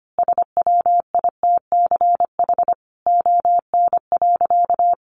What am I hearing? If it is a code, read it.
Morse code, heard as a keyed tone, about 25 wpm, 700 Hz. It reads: SWITCH ON.
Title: A Sound Of Thunder 25wpm